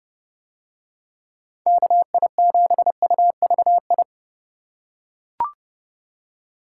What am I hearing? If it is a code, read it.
Morse code, heard as a keyed tone, about 30 wpm, 700 Hz. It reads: KI7UVI